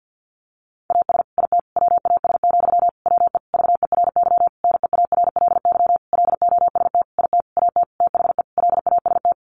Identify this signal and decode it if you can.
Morse code, heard as a keyed tone, about 50 wpm, 700 Hz. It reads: AS IT WAS, WE VERY NEARLY LOST IT AT THE LAST
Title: Sentences from Top 1400 Words 50wpm